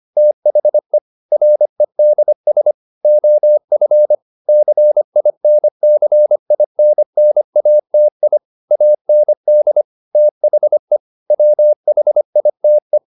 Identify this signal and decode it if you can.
Morse code, heard as a keyed tone, about 25 wpm, 600 Hz. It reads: THE REDS OF CINCINNATI AND THE WHITE
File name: the old man and the sea part 1 25wpm